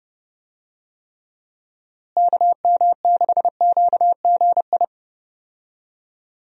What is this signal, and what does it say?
Morse code, heard as a keyed tone, about 30 wpm, 700 Hz. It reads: KM6QGI
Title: USA Call Signs - 6 Characters - Repeated 2x - 30wpm